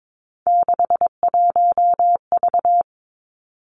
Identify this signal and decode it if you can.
Morse code, heard as a keyed tone, about 22 wpm, 700 Hz. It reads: 61V
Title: Three Challenging Letter-Number Combinations 22wpm